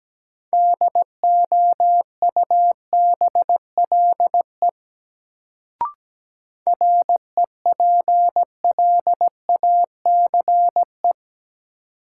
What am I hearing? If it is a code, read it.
Morse code, heard as a keyed tone, about 17 wpm, 700 Hz. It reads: DOUBLE REPLACE
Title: Sentences from Top 1400 Words - New Word Drill - 17wpm